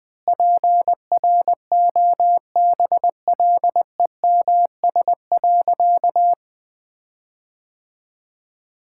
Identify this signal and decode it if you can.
Morse code, heard as a keyed tone, about 20 wpm, 700 Hz. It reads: PROBLEMS.
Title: Difficult Sentences with Hints - 20wpm